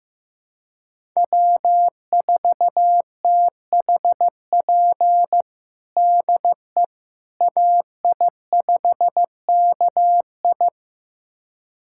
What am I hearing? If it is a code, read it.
Morse code, heard as a keyed tone, about 15 wpm, 700 Hz. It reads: W4THP DE AI5KI
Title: USA Callsign DE Callsign - 5 Characters - 15wpm